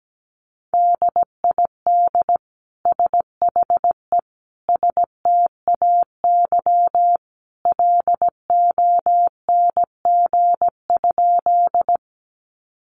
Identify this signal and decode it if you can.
Morse code, heard as a keyed tone, about 17 wpm, 700 Hz. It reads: DID SHE STAY LONG?
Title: Sentences from Top 500 Words 17wpm